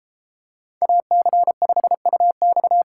Morse code, heard as a keyed tone, about 30 wpm, 700 Hz. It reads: AC5UX